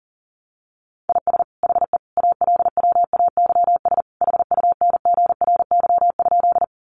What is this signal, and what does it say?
Morse code, heard as a keyed tone, about 40 wpm, 700 Hz. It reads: IS HE ALWAYS HUNGRY?
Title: Sentences from Top 1400 Words 40wpm